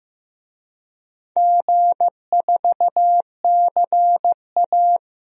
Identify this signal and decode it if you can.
Morse code, heard as a keyed tone, about 15 wpm, 700 Hz. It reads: G4CA
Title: DXCC-100 Call Signs - 4 Characters - Repeated 2x - 15wpm